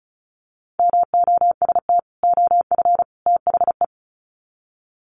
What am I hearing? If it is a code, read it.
Morse code, heard as a keyed tone, about 34 wpm, 700 Hz. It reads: MOST OF THE
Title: Sets of 3 Words 35wpm